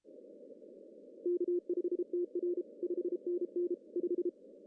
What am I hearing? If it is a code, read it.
Morse code, heard as a keyed tone, about 31 wpm, 350 Hz. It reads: K5TR 5NN 5
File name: CW / cqww-cw-k5tr